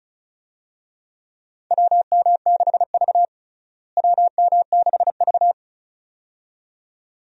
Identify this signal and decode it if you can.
Morse code, heard as a keyed tone, about 36 wpm, 700 Hz. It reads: WM6V WM6V
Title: USA Call Signs - 4 Characters - Repeated 2x - 35wpm